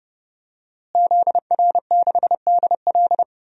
Morse code, heard as a keyed tone, about 31 wpm, 700 Hz. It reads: ZR6DL
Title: DXCC-100 Call Signs - 5 Characters - Repeated 3x - 30wpm